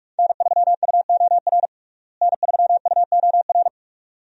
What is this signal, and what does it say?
Morse code, heard as a keyed tone, about 46 wpm, 700 Hz. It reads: N3UOR N3UOR
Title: USA Call Signs - 5 Characters - Repeated 2x - 45wpm